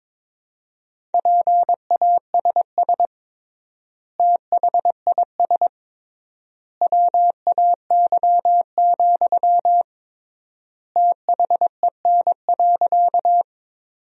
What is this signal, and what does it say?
Morse code, heard as a keyed tone, about 22 wpm, 700 Hz. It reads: PASS THIS WAY, THEN.
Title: 3x Word-Spacing - Sentences from Top 400 Words - 22wpm